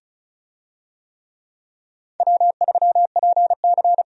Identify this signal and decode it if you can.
Morse code, heard as a keyed tone, about 36 wpm, 700 Hz. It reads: W3PC
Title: USA Call Signs - 4 Characters - 35wpm